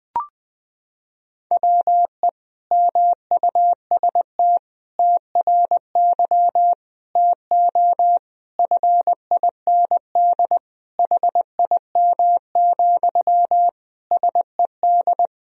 Morse code, heard as a keyed tone, about 20 wpm, 700 Hz. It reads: WE MUST TRY TO FIND HIM, SED
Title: Sentences from Top 400 Words and Abbreviations 20wpm